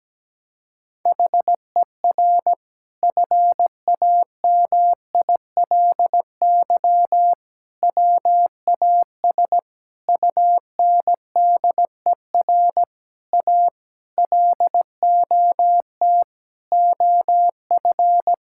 Morse code, heard as a keyed tone, about 17 wpm, 700 Hz. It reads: HER FAMILY WAS UNDER A LOT OF
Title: Sentences from Top 1000 Words - Review Part 6 - 17wpm